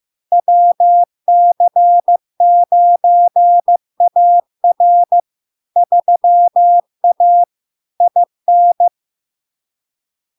Morse code, heard as a keyed tone, about 15 wpm, 700 Hz. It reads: WC9AR 3A IN
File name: LARK-FD-2022-Log-in-CW-15wpm-700-Hz